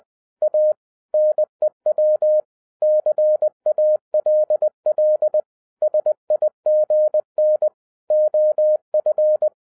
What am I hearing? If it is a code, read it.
Morse code, heard as a keyed tone, about 20 wpm, 610 Hz. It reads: A NEW CALL SIGN OF
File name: 2012-01-mp3 / n05-CAME BACK TO HAM RADIO AFTER 30 YEARS OF INTERVAL-20wpm